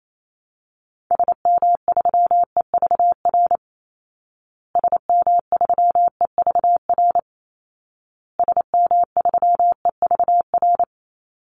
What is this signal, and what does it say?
Morse code, heard as a keyed tone, about 28 wpm, 700 Hz. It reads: SM3EVR SM3EVR SM3EVR
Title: Call Signs Repeated 2x - 28wpm